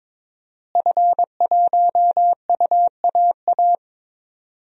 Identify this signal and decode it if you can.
Morse code, heard as a keyed tone, about 22 wpm, 700 Hz. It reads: F1UAA
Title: DXCC-100 Call Signs - 5 Characters - Repeated 3x - 22wpm